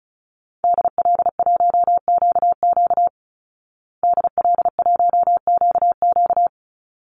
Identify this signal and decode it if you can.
Morse code, heard as a keyed tone, about 34 wpm, 700 Hz. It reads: DL1QQ DL1QQ
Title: Call Signs Repeated 3x - 35wpm